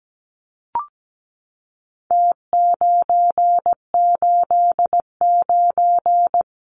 Morse code, heard as a keyed tone, about 17 wpm, 700 Hz. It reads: T989
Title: Numbers - Contest Style - 4-Digit 17wpm